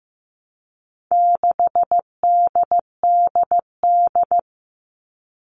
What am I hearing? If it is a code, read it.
Morse code, heard as a keyed tone, about 15 wpm, 700 Hz. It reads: 6DDD
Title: All Cut Numbers - 4-Digit 15wpm